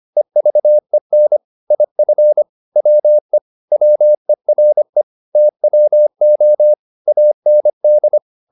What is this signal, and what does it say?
Morse code, heard as a keyed tone, about 25 wpm, 600 Hz. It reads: EVEN IF WE WERE TWO AND